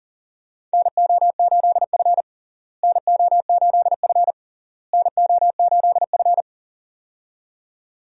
Morse code, heard as a keyed tone, about 40 wpm, 700 Hz. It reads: NO8F NO8F NO8F
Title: USA Call Signs - 4 Characters - Repeated 3x - 40wpm